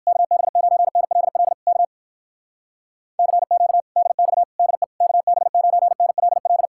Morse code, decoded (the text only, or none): KB9NLL K CQ DX DE KB9NLL